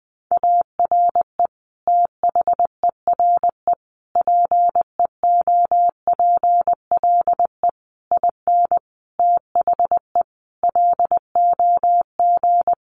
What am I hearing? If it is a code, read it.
Morse code, heard as a keyed tone, about 20 wpm, 700 Hz. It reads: ARE THERE PEOPLE IN THE LOG
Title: Sentences from Top 1400 Words 20wpm